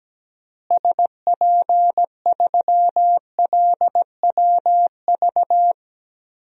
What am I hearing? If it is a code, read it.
Morse code, heard as a keyed tone, about 17 wpm, 700 Hz. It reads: SP3LWV
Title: DXCC-100 Call Signs - 6 Characters - Repeated 2x - 17wpm